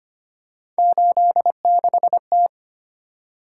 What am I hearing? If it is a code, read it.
Morse code, heard as a keyed tone, about 25 wpm, 700 Hz. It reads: 86T
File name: Three Letter-Number Combinations - 25wpm